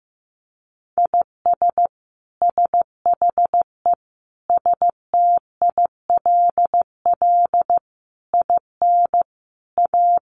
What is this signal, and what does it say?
Morse code, heard as a keyed tone, about 14 wpm, 700 Hz. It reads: IS SHE STILL IN A
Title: Sentences from Top 800 Words 15wpm